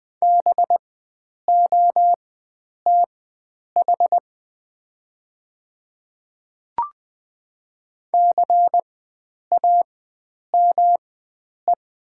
Morse code, heard as a keyed tone, about 20 wpm, spaced as for 9 wpm, 700 Hz. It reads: BOTH CAME
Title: 4x Character Spacing - Top 300 Words - 20wpm